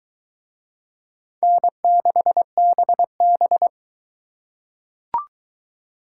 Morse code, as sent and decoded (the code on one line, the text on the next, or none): -. -.... -... -...
N6BB